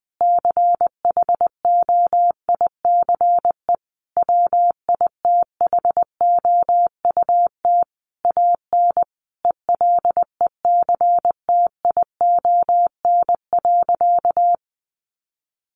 Morse code, heard as a keyed tone, about 20 wpm, 700 Hz. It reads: CHOICE WITHOUT AN ELECTION.
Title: Sentences from Top 1500 Words 20wpm